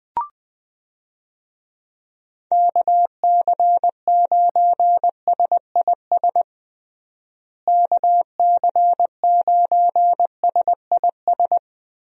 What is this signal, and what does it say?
Morse code, heard as a keyed tone, about 21 wpm, 700 Hz. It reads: KC9SIS KC9SIS
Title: USA Call Signs - 6 Characters - Repeated 2x - 20wpm